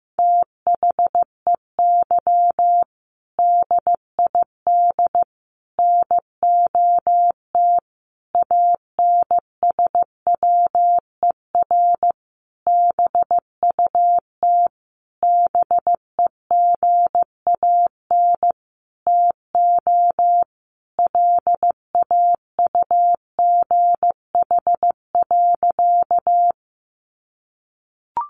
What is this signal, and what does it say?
Morse code, heard as a keyed tone, about 15 wpm, 700 Hz. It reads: THEY DID NOT ANSWER BUT BEGAN TO LAUGH.
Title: Sentences from Top 600 Words 15wpm